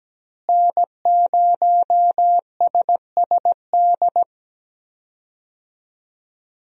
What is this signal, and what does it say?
Morse code, heard as a keyed tone, about 17 wpm, 700 Hz. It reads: N0SSD
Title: USA Call Signs - 5 Characters - Repeated 3x - 17wpm